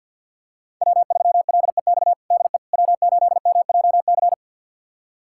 Morse code, heard as a keyed tone, about 52 wpm, 700 Hz. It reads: W3LEX DE W8MJC
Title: USA Callsign DE Callsign - 5 Characters - 50wpm